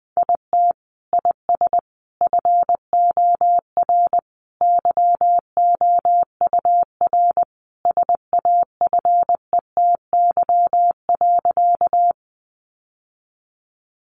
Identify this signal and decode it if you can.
Morse code, heard as a keyed tone, about 20 wpm, 700 Hz. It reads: IT IS FOR YOUR SAFETY.
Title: Sentences from Top 700 Words 20wpm